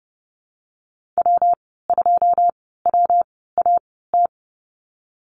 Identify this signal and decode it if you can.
Morse code, heard as a keyed tone, about 31 wpm, spaced as for 17 wpm, 700 Hz. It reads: W2WAT
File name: Lesson 19g - Letter w - Call Signs 30wpm